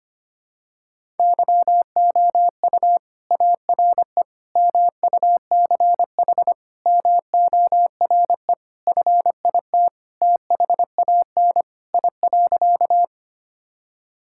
Morse code, heard as a keyed tone, about 25 wpm, 700 Hz. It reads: YOU ARE MUCH MORE FIT THAN I.
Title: Sentences from Top 1000 Words - Review Part 5 - 25wpm